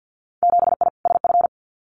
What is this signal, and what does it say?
Morse code, heard as a keyed tone, about 50 wpm, 700 Hz. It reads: 7I SF
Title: Winter Field Day Exchanges Encore 50wpm